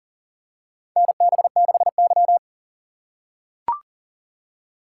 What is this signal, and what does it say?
Morse code, heard as a keyed tone, about 40 wpm, 700 Hz. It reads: NB6Y
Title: USA Call Signs - 6x Character Spacing - 40wpm